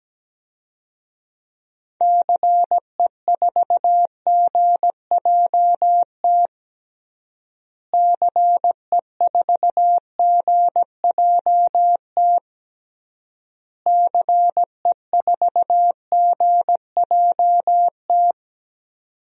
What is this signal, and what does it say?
Morse code, heard as a keyed tone, about 17 wpm, 700 Hz. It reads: CE4GJT CE4GJT CE4GJT
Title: DXCC-100 Call Signs - 6 Characters - Repeated 3x - 17wpm